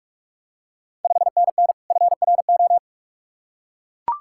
Morse code, heard as a keyed tone, about 44 wpm, 690 Hz. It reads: 5NN FRO
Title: Ontario QSO Party Exchange 45wpm